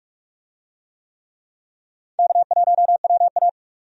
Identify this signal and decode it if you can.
Morse code, heard as a keyed tone, about 46 wpm, 700 Hz. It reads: K1WA
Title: USA Call Signs - 4 Characters - 45wpm